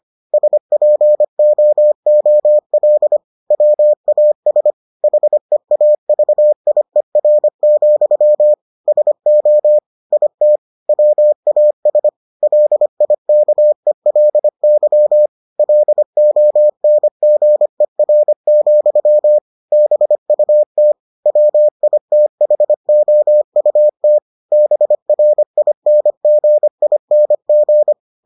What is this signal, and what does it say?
Morse code, heard as a keyed tone, about 25 wpm, 600 Hz. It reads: SPOOL WAS HEAVIER, SO IT WAS LIKELY LONGER, BUT WITHOUT BRINGING